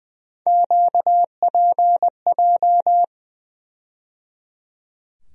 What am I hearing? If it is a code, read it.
Morse code, heard as a keyed tone, about 20 wpm, 700 Hz. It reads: QPJ
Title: USA Call Sign Postfixes - 3 Letters 20wpm